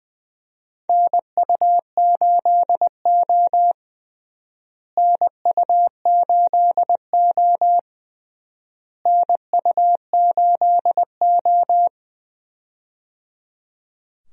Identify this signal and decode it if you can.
Morse code, heard as a keyed tone, about 20 wpm, 700 Hz. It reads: NU8O NU8O NU8O
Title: USA Call Signs - 4 Characters - Repeated 3x - 20wpm